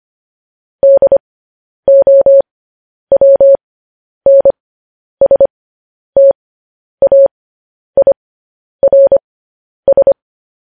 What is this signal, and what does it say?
Morse code, heard as a keyed tone, about 25 wpm, spaced as for 10 wpm, 550 Hz. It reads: DOWNSTAIRS
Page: news headlines in morse code at 10 wpm